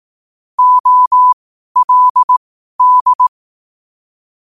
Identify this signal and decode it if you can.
Morse code, heard as a keyed tone, about 18 wpm, spaced as for 12 wpm, 1000 Hz. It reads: OLD
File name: qotdfiles / 12wpm